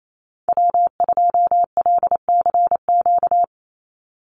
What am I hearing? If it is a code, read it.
Morse code, heard as a keyed tone, about 28 wpm, 700 Hz. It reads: W2LCQ
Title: Call Signs Repeated 3x - 28wpm